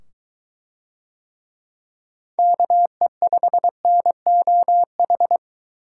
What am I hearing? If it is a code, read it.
Morse code, heard as a keyed tone, about 22 wpm, 700 Hz. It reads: KE5NOH